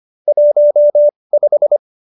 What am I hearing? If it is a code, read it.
Morse code, heard as a keyed tone, about 25 wpm, spaced as for 20 wpm, 590 Hz. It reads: 15
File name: QSO321 20